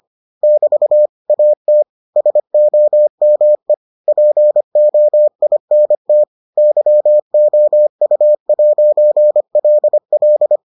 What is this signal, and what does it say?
Morse code, heard as a keyed tone, about 25 wpm, 600 Hz. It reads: <BT> AT SOME POINT YOU'LL